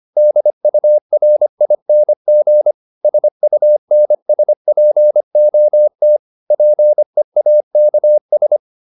Morse code, heard as a keyed tone, about 25 wpm, 600 Hz. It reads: DURING SUNSPOT PEAKS